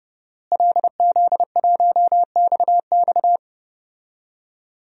Morse code, heard as a keyed tone, about 32 wpm, 700 Hz. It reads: LZ1XX
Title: DXCC-100 Call Signs - 5 Characters - Repeated 2x - 30wpm